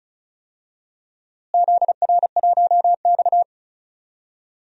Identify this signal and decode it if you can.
Morse code, heard as a keyed tone, about 34 wpm, 700 Hz. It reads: ZR1X
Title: DXCC-100 Call Signs - 4 Characters - Repeated 2x - 35wpm